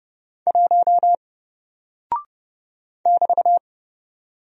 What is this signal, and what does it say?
Morse code, heard as a keyed tone, about 30 wpm, 700 Hz. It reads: 1 <BT>
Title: Single Character (Letter:Number:Punctuation:Prosign) Alt 30wpm